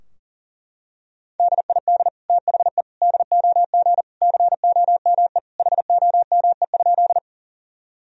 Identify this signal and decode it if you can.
Morse code, heard as a keyed tone, about 37 wpm, 700 Hz. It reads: DID THE DOG COME HOME?
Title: Sentences from Top 1000 Words - Review Part 2 - 40wpm